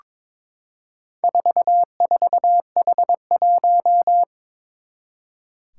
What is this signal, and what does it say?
Morse code, heard as a keyed tone, about 22 wpm, 700 Hz. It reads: 44H1
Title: Four Challenging Letter-Number Combinations 22wpm